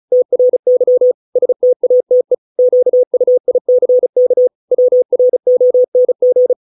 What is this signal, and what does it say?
Morse code, heard as a keyed tone, about 34 wpm, 500 Hz. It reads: TRY STATE QUICK WRONG